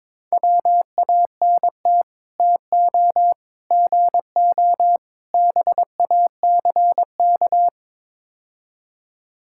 Morse code, heard as a keyed tone, about 23 wpm, 700 Hz. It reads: WANT TO GO BACK
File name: practice / 22wpm / Build A Phrase - 5-Word Phrases - 22wpm